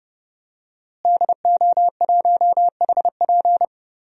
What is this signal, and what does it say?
Morse code, heard as a keyed tone, about 30 wpm, 700 Hz. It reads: DO1HP
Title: DXCC-100 Call Signs - 5 Characters - Repeated 2x - 30wpm